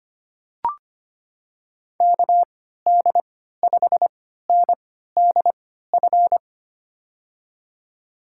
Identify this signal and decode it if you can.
Morse code, heard as a keyed tone, about 25 wpm, spaced as for 14 wpm, 700 Hz. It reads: KD5NDF
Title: USA Call Signs - 6 Characters - 3x Character Spacing - 25wpm